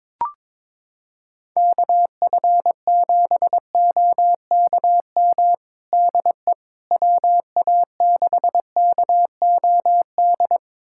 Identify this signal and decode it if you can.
Morse code, heard as a keyed tone, about 22 wpm, 700 Hz. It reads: KF7OKM DE WA6KOD
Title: USA Callsign DE Callsign - 6 Characters - 22wpm